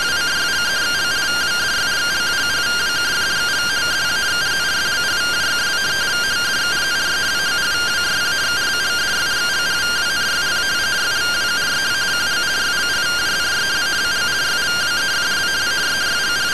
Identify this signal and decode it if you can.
Two-tone teleprinter signal, about 170 Hz shift.